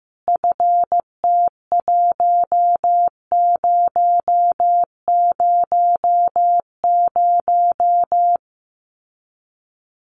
Morse code, heard as a keyed tone, about 15 wpm, 700 Hz. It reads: FT1000
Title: Common QSO Phrases 15wpm